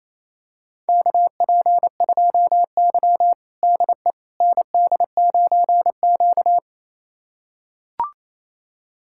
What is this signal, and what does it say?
Morse code, heard as a keyed tone, about 28 wpm, 700 Hz. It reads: KP2Y DE ND9Q
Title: USA Callsign DE Callsign - 4 Characters - 28wpm